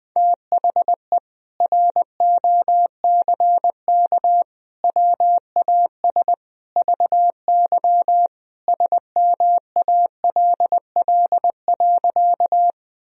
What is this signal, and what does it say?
Morse code, heard as a keyed tone, about 20 wpm, 700 Hz. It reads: THE ROCK WAS VY SMALL.
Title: Sentences from Top 400 Words and Abbreviations 20wpm